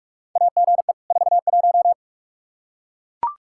Morse code, heard as a keyed tone, about 42 wpm, 700 Hz. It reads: AGE 41
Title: K1USN SST QSO Practice 45wpm